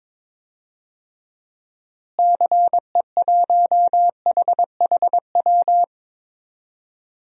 Morse code, heard as a keyed tone, about 22 wpm, 700 Hz. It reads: CE1HHW
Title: DXCC-100 Call Signs - 6 Characters - Repeated 3x - 22wpm